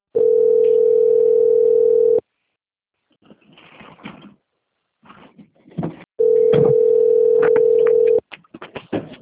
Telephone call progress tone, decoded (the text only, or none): ringback tone